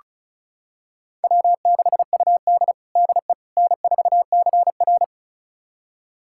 Morse code, heard as a keyed tone, about 34 wpm, 700 Hz. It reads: W6UD DE N4CR
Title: USA Callsign DE Callsign - 4 Characters - 35wpm